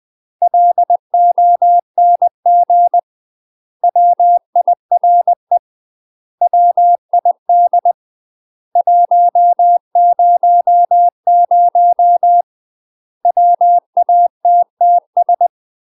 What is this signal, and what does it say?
Morse code, heard as a keyed tone, about 20 wpm, 700 Hz. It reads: LONG WIRE WID 100 WATTS
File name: UK MASTER QSOs TRAINING / 20WPM / UK MASTER QSOs 0070